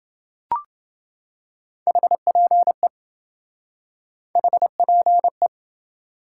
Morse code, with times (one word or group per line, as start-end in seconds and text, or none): 1.9-2.9 HPE
4.4-5.5 HPE